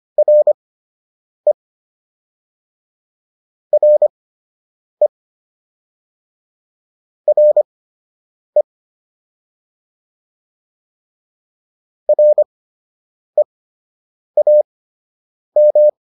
Morse code, heard as a keyed tone, about 25 wpm, spaced as for 8 wpm, 600 Hz. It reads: RE RE RE REAM